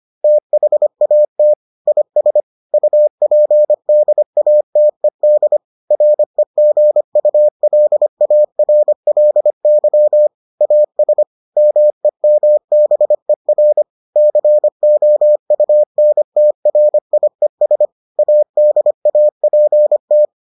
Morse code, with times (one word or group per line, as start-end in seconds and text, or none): 0.2-1.5 THAT
1.9-2.4 IS
2.7-5.6 UPDATED
5.9-10.3 REGULARLY
10.6-11.2 AS
11.6-13.8 MEMBER
14.2-17.9 COUNTRIES
18.2-20.3 ADAPT